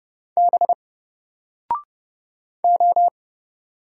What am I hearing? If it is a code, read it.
Morse code, heard as a keyed tone, about 30 wpm, 700 Hz. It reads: B O
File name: Single-Letter-Number-30wpm